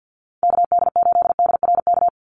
Morse code, heard as a keyed tone, about 49 wpm, 700 Hz. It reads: KD8DRK
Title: USA Call Signs 50wpm